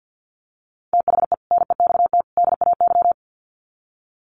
Morse code, heard as a keyed tone, about 50 wpm, 700 Hz. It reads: THE NEXT DAY
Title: Sets of 3 Words 50wpm